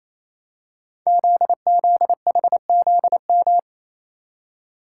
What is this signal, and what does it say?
Morse code, heard as a keyed tone, about 28 wpm, 700 Hz. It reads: ZZHZM
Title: Code Groups - 5 Letters - 28wpm